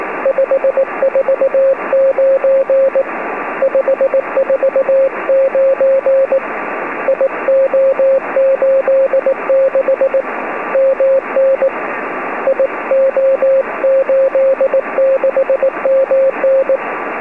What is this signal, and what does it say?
Morse code, heard as a keyed tone, about 18 wpm, 540 Hz, with radio noise. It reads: 549 549 IO86 MN IO86MN